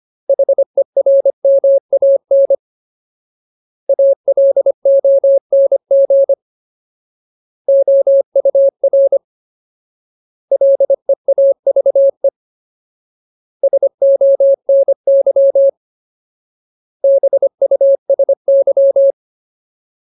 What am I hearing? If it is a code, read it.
Morse code, heard as a keyed tone, about 25 wpm, 550 Hz. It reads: HERMAN ALONG OUR LEAVE SONY BUSY
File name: Ultimate Word List / 25WPM Ultimate